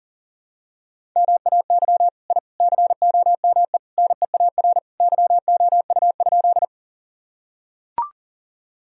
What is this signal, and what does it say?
Morse code, heard as a keyed tone, about 37 wpm, 700 Hz. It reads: MAY I COME NEAR YOU?
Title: Sentences from Top 1000 Words - Review Part 1 - 40wpm